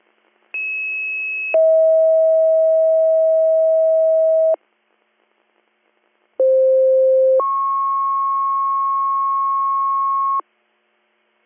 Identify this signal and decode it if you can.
Two-tone page, heard as a one-second tone, then a three-second tone, about 2600 then 640 Hz, about 520 then 1090 Hz. It is heard from 0.5 to 10.4 s.